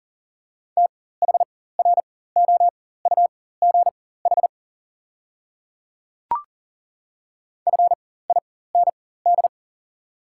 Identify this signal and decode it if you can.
Morse code, heard as a keyed tone, about 40 wpm, spaced as for 19 wpm, 690 Hz. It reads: THROUGH FIND
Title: 4x Character Spacing - Top 200 Words - 40wpm